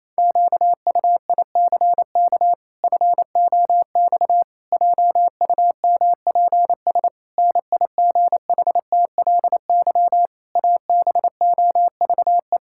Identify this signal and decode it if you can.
Morse code, heard as a keyed tone, about 28 wpm, 700 Hz. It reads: QUICK FOX JUMPS NIGHTLY ABOVE
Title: Pangram Challenge v2 - 28wpm